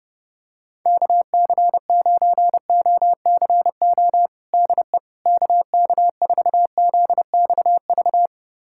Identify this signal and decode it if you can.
Morse code, heard as a keyed tone, about 30 wpm, 700 Hz. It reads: KC9OCO DE KK4ZXV
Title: USA Callsign DE Callsign 30wpm